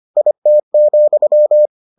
Morse code, heard as a keyed tone, about 25 wpm, 600 Hz. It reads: IT,